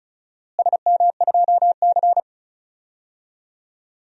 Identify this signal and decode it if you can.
Morse code, heard as a keyed tone, about 34 wpm, 700 Hz. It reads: SM2C
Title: DXCC-100 Call Signs - 4 Characters - Repeated 3x - 35wpm